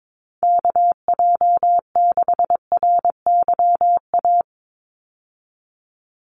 Morse code, heard as a keyed tone, about 22 wpm, 700 Hz. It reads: KJ6RYA
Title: USA Call Signs 22wpm